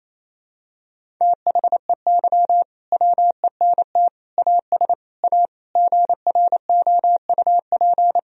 Morse code, heard as a keyed tone, about 27 wpm, 700 Hz. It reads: THEY WENT AS A GROUP
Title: Sentences from Top 1000 Words - Review Part 1 - 28wpm